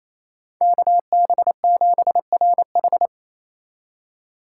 Morse code, heard as a keyed tone, about 28 wpm, 700 Hz. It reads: KB7RH
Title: USA Call Signs - 2x Character Spacing - 28wpm